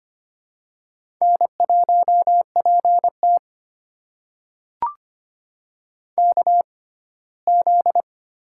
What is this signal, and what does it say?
Morse code, heard as a keyed tone, about 25 wpm, spaced as for 17 wpm, 700 Hz. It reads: N1PT KZ
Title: USA Call Signs - 4 Characters - 6x Character Spacing - 25wpm